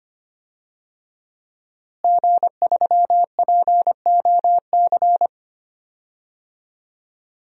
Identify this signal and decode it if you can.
Morse code, heard as a keyed tone, about 25 wpm, 700 Hz. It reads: G3POC
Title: DXCC-100 Call Signs - 5 Characters - Repeated 3x - 25wpm